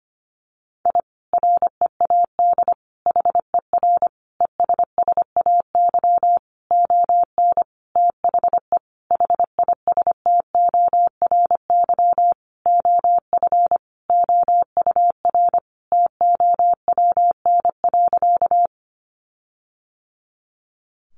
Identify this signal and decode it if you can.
Morse code, heard as a keyed tone, about 25 wpm, 700 Hz. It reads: I READ HER ESSAY ON THE HISTORY OF OUR TOWN.
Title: Sentences from Top 1500 Words 25wpm